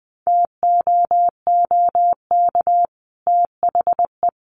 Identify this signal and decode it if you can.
Morse code, heard as a keyed tone, about 20 wpm, 700 Hz. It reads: TOOK THE
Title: Sentences from Top 1300 Words 20wpm